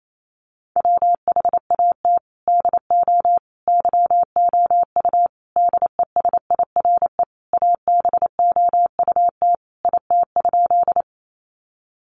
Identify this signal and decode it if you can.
Morse code, heard as a keyed tone, about 28 wpm, 700 Hz. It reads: WHAT DO YOU DESIRE ABOUT IT?
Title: Sentences from Top 1100 Words 28wpm